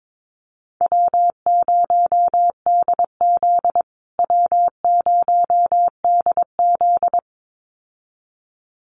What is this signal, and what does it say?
Morse code, heard as a keyed tone, about 22 wpm, 700 Hz. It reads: W0DZ W0DZ
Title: USA Call Signs Repeated 2x 22wpm